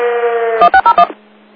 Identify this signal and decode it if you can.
DTMF, heard as telephone touch tones, tones of about 80 ms. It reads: 1601
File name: Old Forge Group 19 30 13